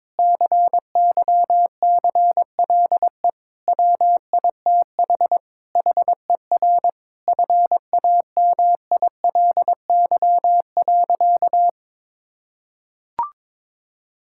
Morse code, heard as a keyed tone, about 21 wpm, 700 Hz. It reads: CYCLE WITH HER FAMILY.
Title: Sentences from Top 1000 Words - Review Part 6 - 22wpm